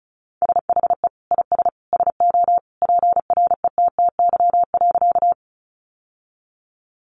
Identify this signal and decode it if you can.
Morse code, heard as a keyed tone, about 34 wpm, 700 Hz. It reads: SHE IS SO PRETTY.
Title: Sentences from Top 1000 Words 35wpm